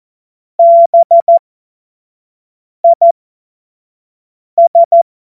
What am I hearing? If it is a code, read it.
Morse code, heard as a keyed tone, about 14 wpm, spaced as for 5 wpm, 680 Hz. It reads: BIS
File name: Morsekurs / Witze / Witz 02. Mai 2023 25 F70